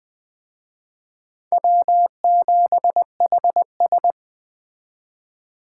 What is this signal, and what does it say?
Morse code, heard as a keyed tone, about 19 wpm, 700 Hz. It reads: W7HS